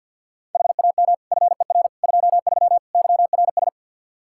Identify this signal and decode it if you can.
Morse code, heard as a keyed tone, about 46 wpm, 700 Hz. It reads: HAM FER 23 YRS
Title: Practice QSO - Exchange 3 - More Info - 50wpm